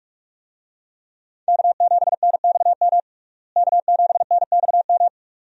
Morse code, heard as a keyed tone, about 43 wpm, 700 Hz. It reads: K7NXM K7NXM
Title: USA Call Signs - 5 Characters - Repeated 2x - 45wpm